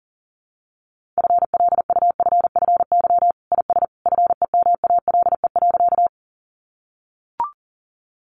Morse code, heard as a keyed tone, about 40 wpm, 700 Hz. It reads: FLUFFY IS FEMALE.
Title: Easy Phrases 40wpm